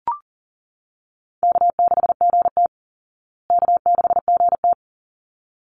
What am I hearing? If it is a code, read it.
Morse code, heard as a keyed tone, about 40 wpm, 700 Hz. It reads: K6GT K6GT
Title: Easy Call Signs Repeated 2x - 40wpm